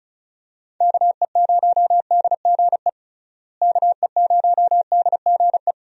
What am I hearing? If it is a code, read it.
Morse code, heard as a keyed tone, about 36 wpm, 700 Hz. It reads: KE0DGE KE0DGE
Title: USA Call Signs - 6 Characters - Repeated 2x - 35wpm